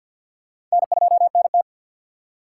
Morse code, heard as a keyed tone, about 54 wpm, 700 Hz. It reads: N2NT
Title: USA Call Signs - 4 Characters - 6x Character Spacing - 50wpm